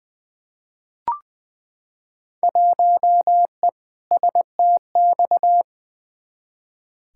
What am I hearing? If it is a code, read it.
Morse code, heard as a keyed tone, about 20 wpm, 700 Hz. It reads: 1E STX
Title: Field Day Exchanges - 20wpm